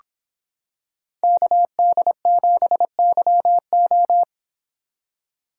Morse code, heard as a keyed tone, about 26 wpm, 700 Hz. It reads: KD7YO